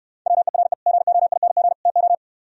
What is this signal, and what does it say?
Morse code, heard as a keyed tone, about 68 wpm, 700 Hz. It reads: WERE GOING TO